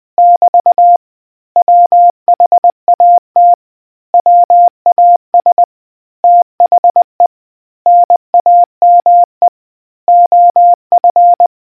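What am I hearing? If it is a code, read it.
Morse code, heard as a keyed tone, about 20 wpm, 700 Hz. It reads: <BT> WHAT WAS THE NAME OF
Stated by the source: Morse code history quiz question from a New Zealand naval communicators association.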